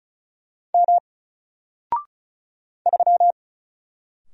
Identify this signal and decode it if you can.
Morse code, heard as a keyed tone, about 36 wpm, 700 Hz. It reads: M 3
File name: Single-Letter-Number-35wpm